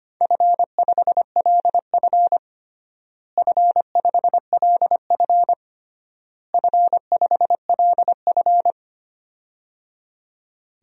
Morse code, heard as a keyed tone, about 25 wpm, 700 Hz. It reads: F5LF F5LF F5LF
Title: DXCC-100 Call Signs - 4 Characters - Repeated 3x - 25wpm